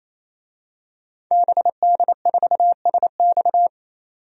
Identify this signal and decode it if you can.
Morse code, heard as a keyed tone, about 27 wpm, 700 Hz. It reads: BD4SX
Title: DXCC-100 Call Signs - 5 Characters - 28wpm